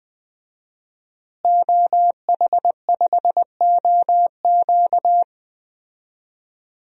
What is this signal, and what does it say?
Morse code, heard as a keyed tone, about 20 wpm, 700 Hz. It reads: OH5OQ
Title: DXCC-100 Call Signs - 5 Characters - Repeated 3x - 20wpm